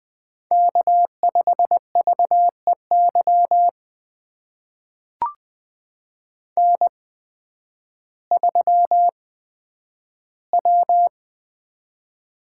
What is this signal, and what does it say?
Morse code, heard as a keyed tone, about 20 wpm, spaced as for 11 wpm, 700 Hz. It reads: K5VEY N3W
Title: USA Call Signs - 5 Characters - 8x Character Spacing - 20wpm